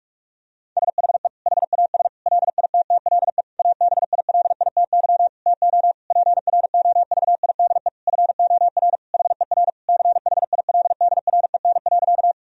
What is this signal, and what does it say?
Morse code, heard as a keyed tone, about 43 wpm, 700 Hz. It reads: SHE HAS LITTLE ABILITY TO PROVIDE FOR HER CHILDREN.
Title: Sentences from Top 1000 Words - Review Part 3 - 45wpm